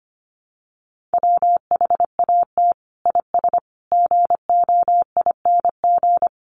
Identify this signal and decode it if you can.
Morse code, heard as a keyed tone, about 25 wpm, 700 Hz. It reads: WHAT IS GOING
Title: Sets of 5 Words Encore2 25wpm